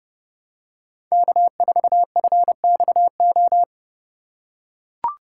Morse code, heard as a keyed tone, about 30 wpm, 700 Hz. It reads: K4FXO